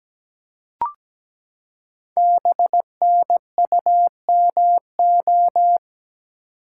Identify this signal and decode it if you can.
Morse code, heard as a keyed tone, about 17 wpm, 700 Hz. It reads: BNUMO
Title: Code Groups - 5 Letters - 17wpm